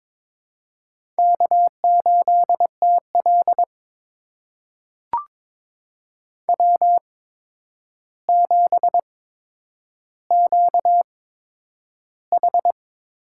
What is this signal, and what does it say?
Morse code, heard as a keyed tone, about 22 wpm, spaced as for 9 wpm, 700 Hz. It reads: K8TL W7QH
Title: USA Call Signs - 4 Characters - 8x Character Spacing - 22wpm